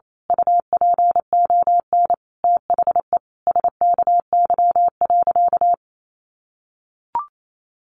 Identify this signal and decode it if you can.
Morse code, heard as a keyed tone, about 28 wpm, 700 Hz. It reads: UPON THE SKY.
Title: A Sound Of Thunder 28wpm